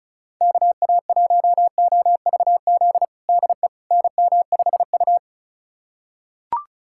Morse code, heard as a keyed tone, about 36 wpm, 700 Hz. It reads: KA1OVZ DE NM5U
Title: USA Callsign DE Callsign 35wpm